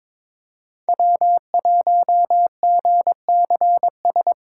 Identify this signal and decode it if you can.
Morse code, heard as a keyed tone, about 22 wpm, 700 Hz. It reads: W1GCS
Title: K1USN SST QSO Practice - 22wpm